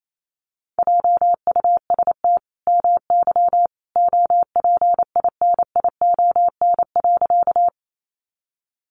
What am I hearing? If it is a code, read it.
Morse code, heard as a keyed tone, about 28 wpm, 700 Hz. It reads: JUST MY OPINION.
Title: Sentences from Top 1200 Words 28wpm